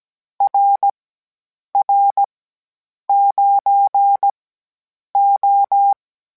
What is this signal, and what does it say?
Morse code, heard as a keyed tone, about 17 wpm, spaced as for 8 wpm, 800 Hz. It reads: RR9O